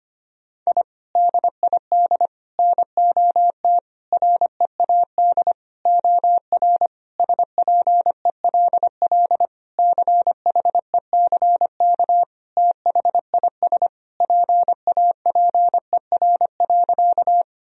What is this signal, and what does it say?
Morse code, heard as a keyed tone, about 25 wpm, 700 Hz. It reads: I DID NOT READ OR SPELL CHECK THIS PAPER.
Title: Sentences from Top 1000 Words - Review Part 3 - 25wpm